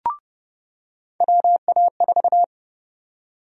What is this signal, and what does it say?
Morse code, heard as a keyed tone, about 30 wpm, 700 Hz. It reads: WA4
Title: USA Call Sign Prefixes - 30wpm